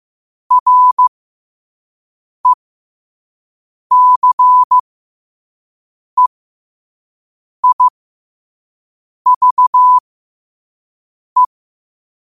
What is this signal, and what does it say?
Morse code, heard as a keyed tone, about 15 wpm, spaced as for 5 wpm, 1000 Hz. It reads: RECEIVE
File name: qotdfiles / 5wpm